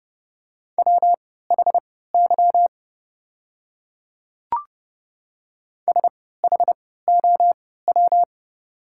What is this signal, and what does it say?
Morse code, heard as a keyed tone, about 29 wpm, spaced as for 17 wpm, 690 Hz. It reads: WHY SHOW
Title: 3x Character Spacing - Top 200 Words - 30wpm